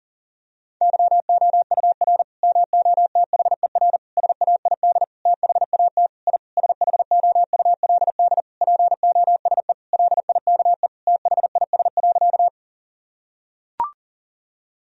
Morse code, heard as a keyed tone, about 40 wpm, 700 Hz. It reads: YOUR MOTHER SAID THAT I SHOULD POSE LIKE THIS.
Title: Sentences from Top 1000 Words - Review Part 2 - 40wpm